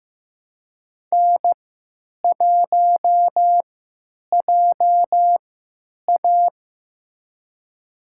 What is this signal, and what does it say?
Morse code, heard as a keyed tone, about 16 wpm, spaced as for 9 wpm, 700 Hz. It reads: N1JA
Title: USA Call Signs - 3x Character Spacing - 15wpm